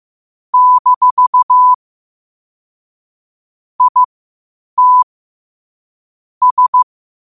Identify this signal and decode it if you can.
Morse code, heard as a keyed tone, about 15 wpm, spaced as for 8 wpm, 1000 Hz. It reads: - IT S
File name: qotdfiles / 8wpm